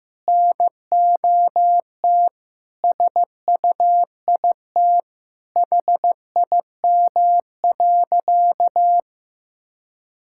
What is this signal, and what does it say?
Morse code, heard as a keyed tone, about 16 wpm, 700 Hz. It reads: NOT SUIT HIM.
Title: Sentences from Top 1600 Words 15wpm